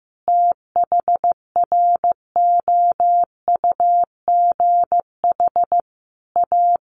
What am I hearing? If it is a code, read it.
Morse code, heard as a keyed tone, about 15 wpm, 700 Hz. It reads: THROUGH A
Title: Sentences from Top 1200 Words 15wpm